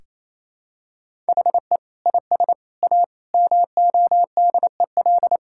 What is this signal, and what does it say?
Morse code, heard as a keyed tone, about 28 wpm, 700 Hz. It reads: HE IS A MODEL
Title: Sentences from Top 1000 Words - Review Part 5 - 28wpm